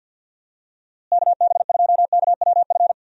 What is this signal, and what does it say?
Morse code, heard as a keyed tone, about 48 wpm, 700 Hz. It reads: KB2KWF
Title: USA Call Signs - 6 Characters - Repeated 2x - 50wpm